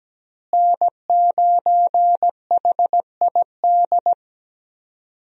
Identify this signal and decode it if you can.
Morse code, heard as a keyed tone, about 17 wpm, 700 Hz. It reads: N9HID
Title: USA Call Signs - 5 Characters - Repeated 2x - 17wpm